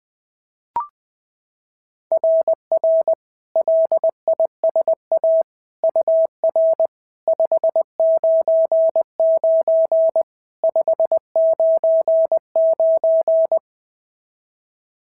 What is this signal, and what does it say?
Morse code, heard as a keyed tone, about 21 wpm, 690 Hz. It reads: RR LISA UR 599 599
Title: SOTA QSO Practice 20wpm